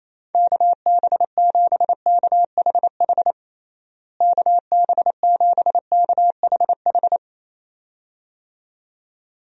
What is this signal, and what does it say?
Morse code, heard as a keyed tone, about 28 wpm, 700 Hz. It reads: KB7KHH KB7KHH
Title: USA Call Signs - 6 Characters - Repeated 2x - 28wpm